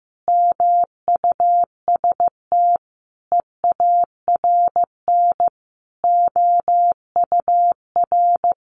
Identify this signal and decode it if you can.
Morse code, heard as a keyed tone, about 15 wpm, 700 Hz. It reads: MUST EARN OUR
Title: Sentences from Top 1500 Words 15wpm